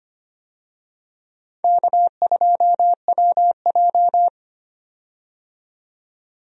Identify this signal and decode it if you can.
Morse code, heard as a keyed tone, about 25 wpm, 700 Hz. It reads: K2WJ
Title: USA Call Signs - 4 Characters - 25wpm